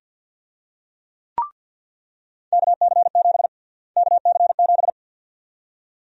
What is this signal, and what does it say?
Morse code, heard as a keyed tone, about 48 wpm, 700 Hz. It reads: KC7 KC7
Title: USA Call Sign Prefixes - Repeated 2x - 50wpm